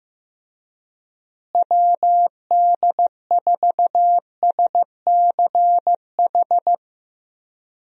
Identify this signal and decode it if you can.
Morse code, heard as a keyed tone, about 16 wpm, 700 Hz. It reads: WD4SCH
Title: USA Call Signs - 6 Characters - Repeated 2x - 15wpm